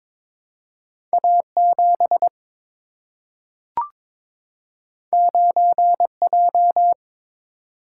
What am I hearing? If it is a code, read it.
Morse code, heard as a keyed tone, about 22 wpm, 700 Hz. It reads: A 7 9 J
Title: Single-Letter Single-Number 22wpm